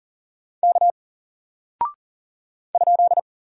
Morse code, heard as a keyed tone, about 40 wpm, 700 Hz. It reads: K ?
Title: Single Character (Letter:Number:Punctuation:Prosign) Alt 40wpm